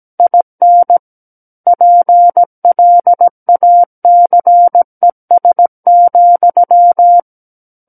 Morse code, heard as a keyed tone, about 17 wpm, 700 Hz. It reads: IN PLACES,
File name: Book One / 08-Chapter 8 - 17 WPM